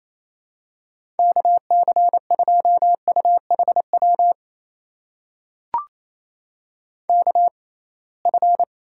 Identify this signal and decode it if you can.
Morse code, heard as a keyed tone, about 28 wpm, spaced as for 21 wpm, 700 Hz. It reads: KC2UHW KF
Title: USA Call Signs - 6 Characters - 6x Character Spacing - 28wpm